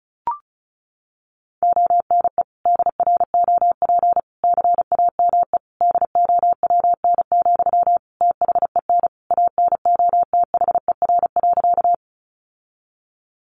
Morse code, heard as a keyed tone, about 34 wpm, 700 Hz. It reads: ONE DROP CAME DOWN, THEN ANOTHER.
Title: Sentences from Top 1400 Words 35wpm